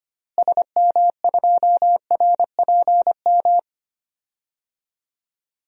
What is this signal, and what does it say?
Morse code, heard as a keyed tone, about 25 wpm, 700 Hz. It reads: SM2RPM
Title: DXCC-100 Call Signs - 6 Characters - Repeated 3x - 25wpm